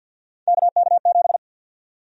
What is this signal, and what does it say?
Morse code, heard as a keyed tone, about 52 wpm, 700 Hz. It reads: KK7
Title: USA Call Sign Prefixes - 50wpm